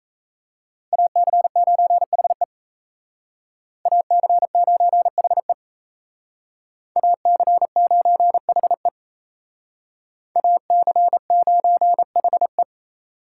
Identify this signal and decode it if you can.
Morse code, heard as a keyed tone, about 36 wpm, 700 Hz. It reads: AC9HE AC9HE AC9HE AC9HE